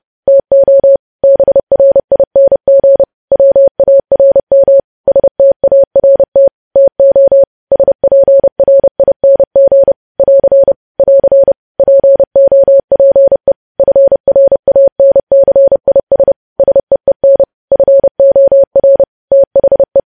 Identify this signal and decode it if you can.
Morse code, heard as a keyed tone, about 30 wpm, 550 Hz. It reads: TO BRING WARM START TO SPRING <AR> <AR> POPE FRANCIS SEEN FOR THE
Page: News Headlines in Morse Code at 30 WPM